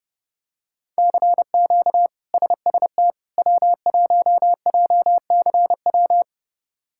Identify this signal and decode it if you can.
Morse code, heard as a keyed tone, about 32 wpm, 700 Hz. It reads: CQ SST W1JCW
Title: K1USN SST QSO Practice - 30wpm